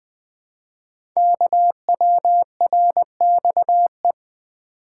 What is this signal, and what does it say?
Morse code, heard as a keyed tone, about 20 wpm, 700 Hz. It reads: KWRXE
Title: Code Groups - 5 Letters - 20wpm